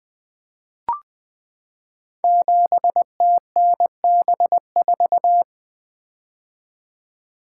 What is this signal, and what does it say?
Morse code, heard as a keyed tone, about 20 wpm, 700 Hz. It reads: 7TNB4
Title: Code Groups - 5 Letter:Numbers - 20wpm